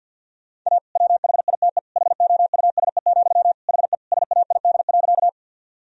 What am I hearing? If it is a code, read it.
Morse code, heard as a keyed tone, about 45 wpm, 700 Hz. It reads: A WHITE HOUSE, HE SAID.
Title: Sentences from Top 1000 Words - Review Part 1 - 50wpm